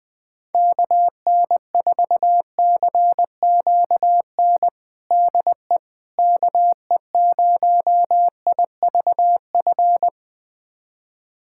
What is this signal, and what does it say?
Morse code, heard as a keyed tone, about 20 wpm, 700 Hz. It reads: KN4CQN DE KE0IVF
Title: USA Callsign DE Callsign - 6 Characters - 20wpm